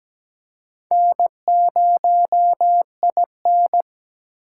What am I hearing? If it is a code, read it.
Morse code, heard as a keyed tone, about 21 wpm, 700 Hz. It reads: N0IN
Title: Speed Racing - USA 1x2 Call Signs 17wpm